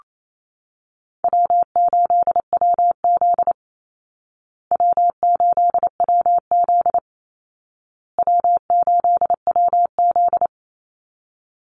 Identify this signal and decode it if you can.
Morse code, heard as a keyed tone, about 28 wpm, 700 Hz. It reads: W8WZ W8WZ W8WZ
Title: Easy Call Signs Repeated 3x - 28wpm